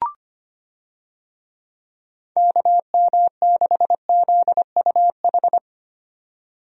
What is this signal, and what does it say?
Morse code, heard as a keyed tone, about 25 wpm, 700 Hz. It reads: KM6ZUH
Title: USA Call Signs - 6 Characters - 25wpm